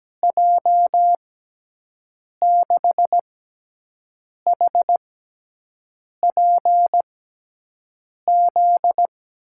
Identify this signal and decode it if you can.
Morse code, heard as a keyed tone, about 17 wpm, spaced as for 6 wpm, 700 Hz. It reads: J6HPZ
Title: USA Call Signs - 6 Characters - 6x Character Spacing - 17wpm